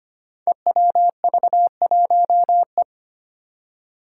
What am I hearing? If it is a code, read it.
Morse code, heard as a keyed tone, about 25 wpm, 700 Hz. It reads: EWV1E
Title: Five Challenging Letter-Number Combinations 25wpm